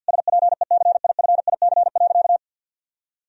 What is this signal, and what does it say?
Morse code, heard as a keyed tone, about 49 wpm, 700 Hz. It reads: SPECIFIC.
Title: Sentences from Top 1000 Words - Review Part 3 - 50wpm